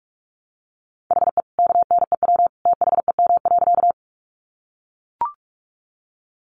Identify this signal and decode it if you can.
Morse code, heard as a keyed tone, about 44 wpm, 700 Hz. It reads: HE KNEW THEM.
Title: Sentences from Top 400 Words 45wpm